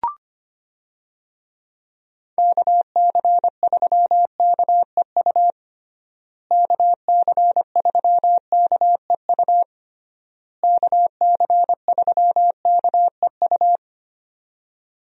Morse code, heard as a keyed tone, about 25 wpm, 700 Hz. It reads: KC3KEU KC3KEU KC3KEU
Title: USA Call Signs - 6 Characters - Repeated 3x - 25wpm